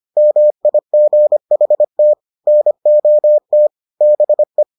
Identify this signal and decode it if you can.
Morse code, heard as a keyed tone, about 25 wpm, 600 Hz. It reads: MIGHT NOT BE